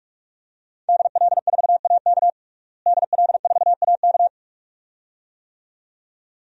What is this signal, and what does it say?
Morse code, heard as a keyed tone, about 45 wpm, 700 Hz. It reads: DL4AK DL4AK
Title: DXCC-100 Call Signs - 5 Characters - Repeated 2x - 45wpm